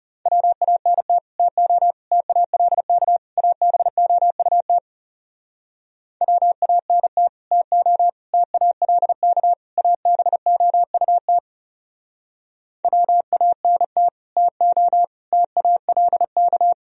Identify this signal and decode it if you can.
Morse code, heard as a keyed tone, about 37 wpm, 700 Hz. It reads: WANT TO TALK ABOUT WANT TO TALK ABOUT WANT TO TALK
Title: Speed-Racing - Sets of 4 Words 30wpm